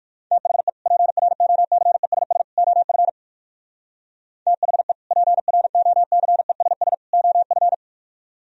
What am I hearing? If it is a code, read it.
Morse code, heard as a keyed tone, about 64 wpm, spaced as for 56 wpm, 700 Hz. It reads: THE PROCESS OF THE PROCESS OF
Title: Speed-Racing - Sets of 3 Words 45wpm